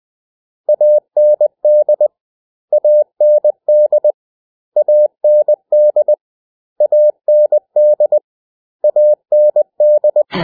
Morse code, heard as a keyed tone, about 20 wpm, 600 Hz. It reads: AND AND AND AND AND
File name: cw / 100 Common English Words